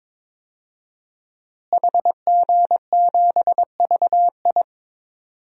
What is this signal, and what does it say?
Morse code, heard as a keyed tone, about 21 wpm, 700 Hz. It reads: HG7VI
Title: DXCC-100 Call Signs - 5 Characters - Repeated 3x - 22wpm